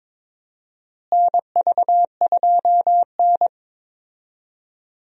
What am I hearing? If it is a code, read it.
Morse code, heard as a keyed tone, about 22 wpm, 700 Hz. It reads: NV2N
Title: USA Call Signs - 4 Characters - Repeated 3x - 22wpm